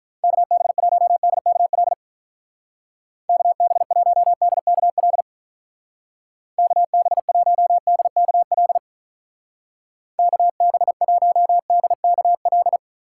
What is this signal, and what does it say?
Morse code, heard as a keyed tone, about 44 wpm, 700 Hz. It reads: KB1DKL KB1DKL KB1DKL KB1DKL